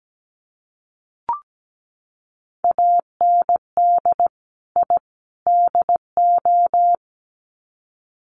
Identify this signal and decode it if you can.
Morse code, heard as a keyed tone, about 17 wpm, 700 Hz. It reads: AND I DO
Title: Sets of 3 Words 17wpm